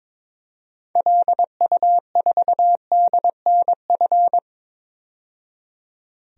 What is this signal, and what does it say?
Morse code, heard as a keyed tone, about 22 wpm, 700 Hz. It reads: LU4DNF
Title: DXCC-100 Call Signs - 6 Characters - 22wpm